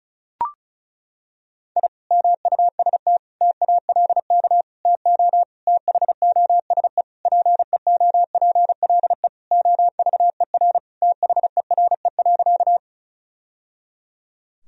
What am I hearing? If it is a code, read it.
Morse code, heard as a keyed tone, about 36 wpm, 700 Hz. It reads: I MUST TALK TO THOSE PEOPLE OVER THERE.
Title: Sentences from Top 1000 Words - Review Part 1 - 35wpm